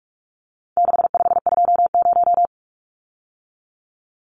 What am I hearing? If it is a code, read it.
Morse code, heard as a keyed tone, about 44 wpm, 700 Hz. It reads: 6520
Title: Numbers - 4-Digit 45wpm